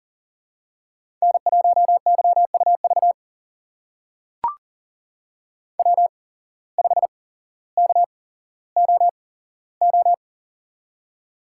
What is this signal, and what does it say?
Morse code, heard as a keyed tone, about 39 wpm, spaced as for 17 wpm, 700 Hz. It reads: N1YUV W5KOO
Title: USA Call Signs - 5 Characters - 8x Character Spacing - 40wpm